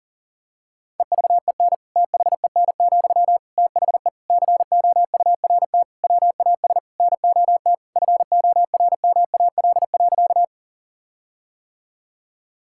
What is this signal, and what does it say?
Morse code, heard as a keyed tone, about 40 wpm, 700 Hz. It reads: EVEN THEN, THE COURT WAS NOT FORMAL.
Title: Sentences from Top 1600 Words 40wpm